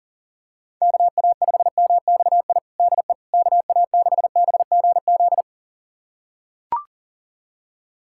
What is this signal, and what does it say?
Morse code, heard as a keyed tone, about 37 wpm, 700 Hz. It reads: KA5MXI DE KA6BGZ
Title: USA Callsign DE Callsign - 6 Characters - 40wpm